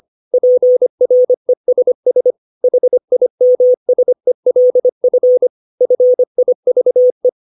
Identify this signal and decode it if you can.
Morse code, heard as a keyed tone, about 25 wpm, 500 Hz. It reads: PRESS HIMSELF FIVE
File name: 632 words / 632 Words 25WPM